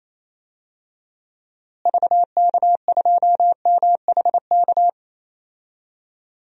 Morse code, heard as a keyed tone, about 28 wpm, 700 Hz. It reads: VK2MHK
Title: DXCC-100 Call Signs - 6 Characters - Repeated 2x - 28wpm